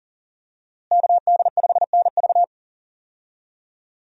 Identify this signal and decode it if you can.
Morse code, heard as a keyed tone, about 39 wpm, 700 Hz. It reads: KD5NV
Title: USA Call Signs - 5 Characters - Repeated 3x - 40wpm